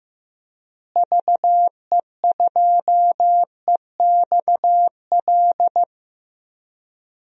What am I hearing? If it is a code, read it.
Morse code, heard as a keyed tone, about 15 wpm, 700 Hz. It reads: VE2EXL
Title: DXCC-100 Call Signs - 6 Characters - Repeated 3x - 15wpm